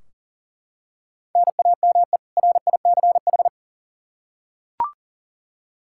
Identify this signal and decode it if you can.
Morse code, heard as a keyed tone, about 39 wpm, 700 Hz. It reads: NAME RICH
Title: K1USN SST QSO Practice 40wpm